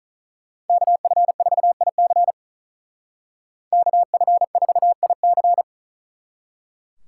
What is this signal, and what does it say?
Morse code, heard as a keyed tone, about 46 wpm, 700 Hz. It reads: KF4IC KF4IC